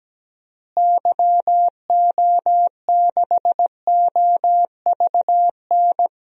Morse code, heard as a keyed tone, about 17 wpm, 700 Hz. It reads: YO6OVN
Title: DXCC-100 Call Signs - 6 Characters - Repeated 2x - 17wpm